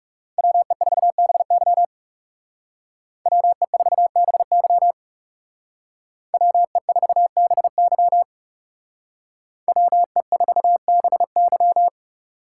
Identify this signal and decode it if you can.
Morse code, heard as a keyed tone, about 37 wpm, 700 Hz. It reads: WE4BY WE4BY WE4BY WE4BY
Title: Speed Racing - USA 2x2 Call Signs 30wpm